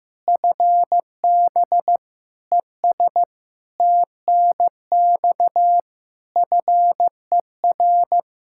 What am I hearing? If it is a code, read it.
Morse code, heard as a keyed tone, about 15 wpm, 700 Hz. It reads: FB ES TNX FER
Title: Practice QSO - Exchange 2 - Info - 15wpm